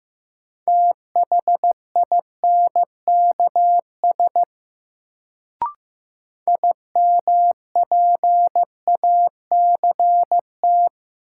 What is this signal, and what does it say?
Morse code, heard as a keyed tone, about 15 wpm, 700 Hz. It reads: THINKS IMPACT
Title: Sentences from Top 1000 Words - Review Part 6 - New Word Drill - 15wpm